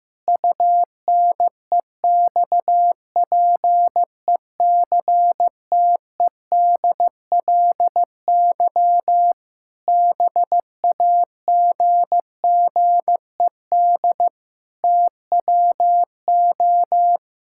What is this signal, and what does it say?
Morse code, heard as a keyed tone, about 15 wpm, 700 Hz. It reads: UNEXPECTEDLY BAGGED TWO
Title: Pangram Challenge v2 - 15wpm